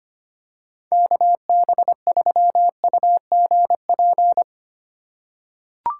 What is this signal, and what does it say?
Morse code, heard as a keyed tone, about 25 wpm, 700 Hz. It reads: KB3UGP
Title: USA Call Signs - 4x Character Spacing - 25wpm